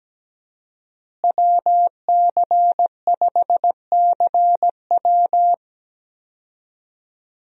Morse code, heard as a keyed tone, about 17 wpm, 700 Hz. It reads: WC5CW
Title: USA Call Signs - 5 Characters - Repeated 3x - 17wpm